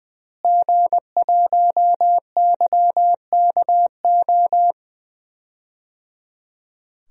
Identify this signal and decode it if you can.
Morse code, heard as a keyed tone, about 20 wpm, 700 Hz. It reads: G1YKO
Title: DXCC-100 Call Signs - 5 Characters - Repeated 3x - 20wpm